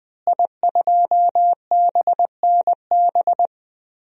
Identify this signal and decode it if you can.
Morse code, heard as a keyed tone, about 20 wpm, 700 Hz. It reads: I2BNB
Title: DXCC-100 Call Signs - 5 Characters - Repeated 2x - 20wpm